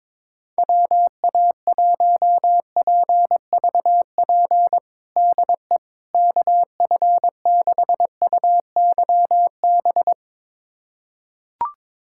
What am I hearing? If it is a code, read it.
Morse code, heard as a keyed tone, about 22 wpm, 700 Hz. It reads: WA1PVP DE KF6UYB
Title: USA Callsign DE Callsign - 6 Characters - 22wpm